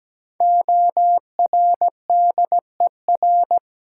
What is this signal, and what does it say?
Morse code, heard as a keyed tone, about 17 wpm, 700 Hz. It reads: ORDER
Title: Sentences from Top 400 Words - New Word Drill - 17wpm